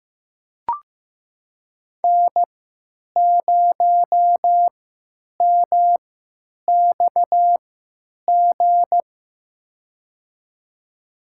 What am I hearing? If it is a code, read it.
Morse code, heard as a keyed tone, about 15 wpm, spaced as for 9 wpm, 700 Hz. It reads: N0MXG
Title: USA Call Signs - 3x Character Spacing - 15wpm